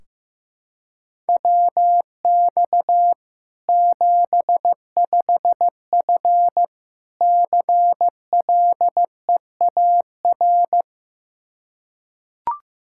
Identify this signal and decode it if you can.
Morse code, heard as a keyed tone, about 15 wpm, 700 Hz. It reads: WX 75F CLEAR
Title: Practice QSO - Exchange 2 - Info - 15wpm